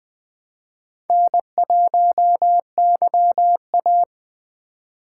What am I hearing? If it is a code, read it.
Morse code, heard as a keyed tone, about 21 wpm, 700 Hz. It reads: N1YA
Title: Speed Racing - USA 1x2 Call Signs 17wpm